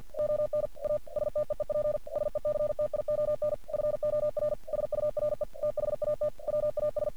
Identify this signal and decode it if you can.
Morse code, heard as a keyed tone, about 46 wpm, 600 Hz. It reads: ON A STEEP SECTION FOR SURE THAT WAS
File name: archived / CRSnet